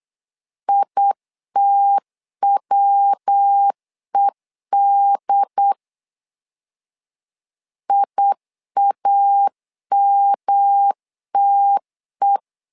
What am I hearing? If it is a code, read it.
Morse code, heard as a keyed tone, about 8 wpm, 800 Hz. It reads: ITWED IAMTE
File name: LEZIONI / 3 MWRO / 3 MW 8wpm 2501